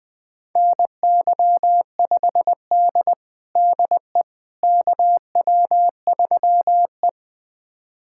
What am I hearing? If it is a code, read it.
Morse code, heard as a keyed tone, about 20 wpm, 700 Hz. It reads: NY5D DE KW3E
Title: USA Callsign DE Callsign - 4 Characters - 20wpm